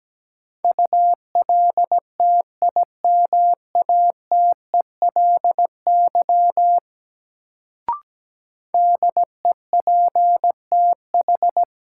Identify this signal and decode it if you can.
Morse code, heard as a keyed tone, about 17 wpm, 700 Hz. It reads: ULTIMATELY DEPTH
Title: Sentences from Top 1200 Words - New Word Drill Alt - 17wpm